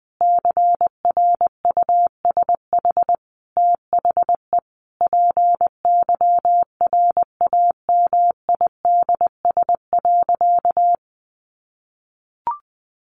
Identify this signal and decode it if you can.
Morse code, heard as a keyed tone, about 20 wpm, 700 Hz. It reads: CRUSH THE PYRAMIDS.
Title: A Sound Of Thunder 20wpm